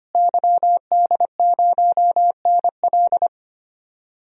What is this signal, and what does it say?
Morse code, heard as a keyed tone, about 25 wpm, 700 Hz. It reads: YD0NL
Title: DXCC-100 Call Signs - 5 Characters - Repeated 2x - 25wpm